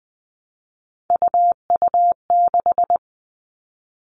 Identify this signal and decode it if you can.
Morse code, heard as a keyed tone, about 20 wpm, 700 Hz. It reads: UU6
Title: Three Challenging Letter-Number Combinations 20wpm